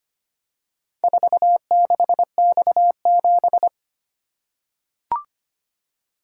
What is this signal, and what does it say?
Morse code, heard as a keyed tone, about 25 wpm, 700 Hz. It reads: 46X7
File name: practice / 25wpm / Four Letter-Number Combinations - 25wpm